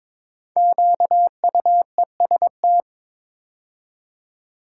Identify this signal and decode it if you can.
Morse code, heard as a keyed tone, about 22 wpm, 690 Hz. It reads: QUEST
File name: 5 Letter Words - First 3 Letters Given - 22wpm